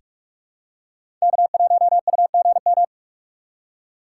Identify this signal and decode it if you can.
Morse code, heard as a keyed tone, about 42 wpm, 700 Hz. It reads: K1UGM
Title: USA Call Signs - 5 Characters - 6x Character Spacing - 45wpm